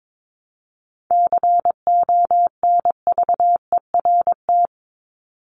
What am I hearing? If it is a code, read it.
Morse code, heard as a keyed tone, about 22 wpm, 700 Hz. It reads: CONVERT
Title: Top 1751-2000 Words 22wpm